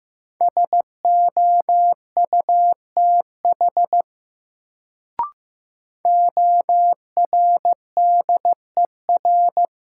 Morse code, heard as a keyed tone, about 15 wpm, 700 Hz. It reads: SOUTH ORDER
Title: Sentences from Top 400 Words - New Word Drill - 15wpm